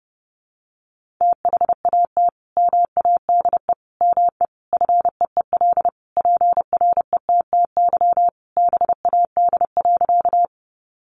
Morse code, heard as a keyed tone, about 30 wpm, 700 Hz. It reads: THAT MADE ME FEEL PRETTY BAD.
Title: Sentences from Top 1000 Words 30wpm